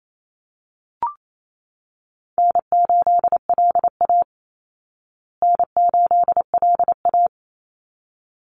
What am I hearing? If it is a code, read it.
Morse code, heard as a keyed tone, about 28 wpm, 700 Hz. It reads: N8LA N8LA
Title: Easy Call Signs Repeated 2x - 28wpm